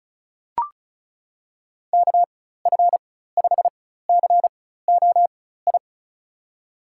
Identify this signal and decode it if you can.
Morse code, heard as a keyed tone, about 34 wpm, spaced as for 16 wpm, 700 Hz. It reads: KF5COI
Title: USA Call Signs - 6 Characters - 4x Character Spacing - 35wpm